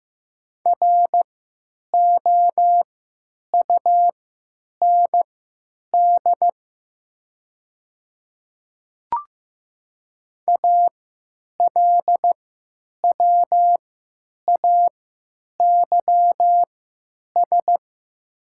Morse code, heard as a keyed tone, about 15 wpm, spaced as for 9 wpm, 690 Hz. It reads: ROUND ALWAYS
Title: 3x Character Spacing - Top 300 Words - 15wpm